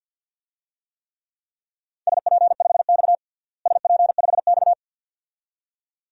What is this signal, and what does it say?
Morse code, heard as a keyed tone, about 47 wpm, 700 Hz. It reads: SP5X SP5X
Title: DXCC-100 Call Signs - 4 Characters - Repeated 2x - 50wpm